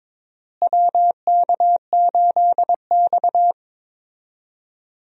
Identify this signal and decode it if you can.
Morse code, heard as a keyed tone, about 22 wpm, 700 Hz. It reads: WK8X
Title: USA Call Signs - 4 Characters - Repeated 3x - 22wpm